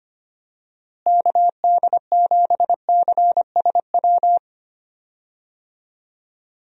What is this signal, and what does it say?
Morse code, heard as a keyed tone, about 25 wpm, 700 Hz. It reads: KD7CSW
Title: USA Call Signs - 6 Characters - Repeated 3x - 25wpm